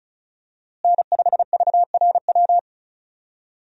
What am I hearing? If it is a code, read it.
Morse code, heard as a keyed tone, about 36 wpm, 700 Hz. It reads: N5VRW
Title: USA Call Signs - 5 Characters - Repeated 3x - 35wpm